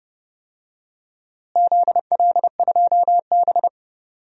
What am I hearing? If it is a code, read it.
Morse code, heard as a keyed tone, about 30 wpm, 700 Hz. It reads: ZL2B
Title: DXCC-100 Call Signs - 4 Characters - Repeated 2x - 30wpm